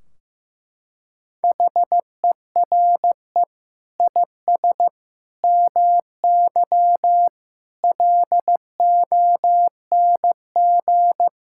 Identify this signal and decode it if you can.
Morse code, heard as a keyed tone, about 15 wpm, 700 Hz. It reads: HERE IS MY LONG
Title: Sentences from Top 1600 Words 15wpm